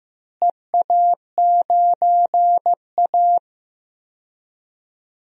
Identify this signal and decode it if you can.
Morse code, heard as a keyed tone, about 15 wpm, 700 Hz. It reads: EA9A
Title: DXCC-100 Call Signs - 4 Characters - Repeated 2x - 15wpm